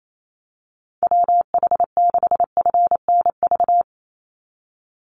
Morse code, heard as a keyed tone, about 28 wpm, 700 Hz. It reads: WH6FNV
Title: USA Call Signs 28wpm